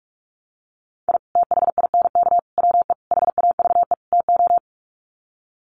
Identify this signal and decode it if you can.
Morse code, heard as a keyed tone, about 44 wpm, 700 Hz. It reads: I THINK WE HAVE TO
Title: Sets of 5 Words 45wpm